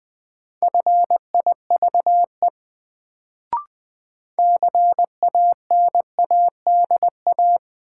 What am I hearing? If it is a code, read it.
Morse code, heard as a keyed tone, about 20 wpm, 700 Hz. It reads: FIVE CANADA
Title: Sets of 5 Words - New Word Drill - 20wpm